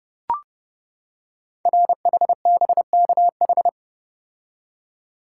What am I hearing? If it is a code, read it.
Morse code, heard as a keyed tone, about 29 wpm, 700 Hz. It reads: RHBKH
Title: Code Groups - 5 Letters - 30wpm